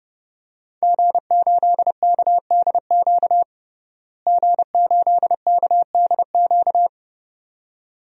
Morse code, heard as a keyed tone, about 32 wpm, 700 Hz. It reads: G8KDQ G8KDQ
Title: DXCC-100 Call Signs - 5 Characters - Repeated 2x - 30wpm